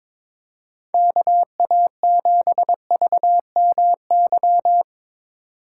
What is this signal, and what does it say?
Morse code, heard as a keyed tone, about 24 wpm, 700 Hz. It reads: KA7VMY
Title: Speed Racing - USA Call Signs 22wpm